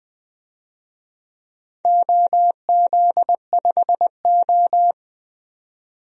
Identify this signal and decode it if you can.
Morse code, heard as a keyed tone, about 21 wpm, 700 Hz. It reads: OZ5O
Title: DXCC-100 Call Signs - 4 Characters - 20wpm